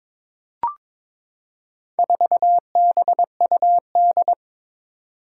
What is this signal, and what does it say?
Morse code, heard as a keyed tone, about 22 wpm, 700 Hz. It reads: 4BUD
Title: Four Challenging Letter-Number Combinations 22wpm